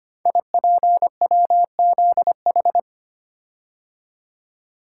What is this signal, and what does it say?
Morse code, heard as a keyed tone, about 25 wpm, 700 Hz. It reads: IPWZH
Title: Code Groups - 5 Letters - 25wpm